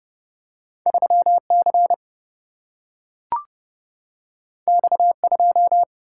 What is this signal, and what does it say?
Morse code, heard as a keyed tone, about 30 wpm, 700 Hz. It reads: 3 C X 2
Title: Single-Letter Single-Number 30wpm